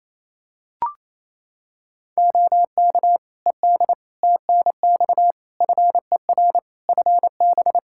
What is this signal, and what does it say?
Morse code, heard as a keyed tone, about 28 wpm, 700 Hz. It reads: OK ED TNX FER FB
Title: K1USN SST QSO Practice 28wpm